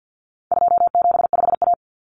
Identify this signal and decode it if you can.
Morse code, heard as a keyed tone, about 50 wpm, 700 Hz. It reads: 275A
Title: Common Cut Numbers - 4-Digit 50wpm